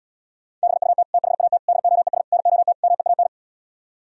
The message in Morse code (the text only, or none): BUT THAT DOES NOT MEAN